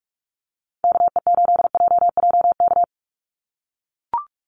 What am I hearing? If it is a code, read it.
Morse code, heard as a keyed tone, about 44 wpm, 700 Hz. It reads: KE8JJK
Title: USA Call Signs 45wpm